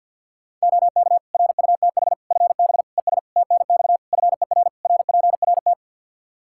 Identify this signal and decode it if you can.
Morse code, heard as a keyed tone, about 50 wpm, 700 Hz. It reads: OK RUTH FB ES TNX FER RPRT
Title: Practice QSO - Exchange 2 - Info - 50wpm